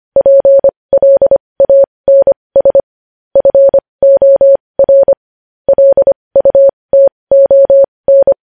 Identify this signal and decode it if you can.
Morse code, heard as a keyed tone, about 25 wpm, spaced as for 20 wpm, 550 Hz. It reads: PLANS FOR LUTON
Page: News Headlines in Morse Code at 20 WPM